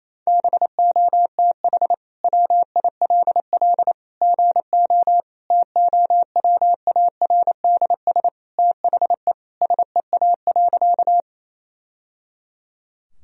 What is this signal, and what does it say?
Morse code, heard as a keyed tone, about 28 wpm, 700 Hz. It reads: BOTH WILL GO TOWARDS THE SEA.
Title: Sentences from Top 1000 Words - Review Part 2 - 28wpm